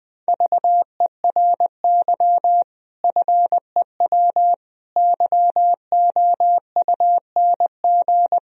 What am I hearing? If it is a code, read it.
Morse code, heard as a keyed tone, about 20 wpm, 700 Hz. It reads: VERY FEW YOUNG
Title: Sentences from Top 1000 Words - Review Part 1 - 20wpm